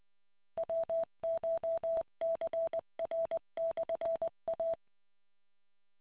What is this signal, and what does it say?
Morse code, heard as a keyed tone, about 20 wpm, 660 Hz. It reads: WCHCR/A